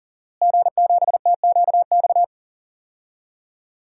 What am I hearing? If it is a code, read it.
Morse code, heard as a keyed tone, about 40 wpm, 700 Hz. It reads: G7TQX
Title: DXCC-100 Call Signs - 5 Characters - Repeated 3x - 40wpm